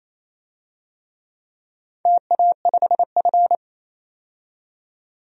Morse code, heard as a keyed tone, about 27 wpm, 690 Hz. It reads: TA5F
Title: DXCC-100 Call Signs - 4 Characters - 28wpm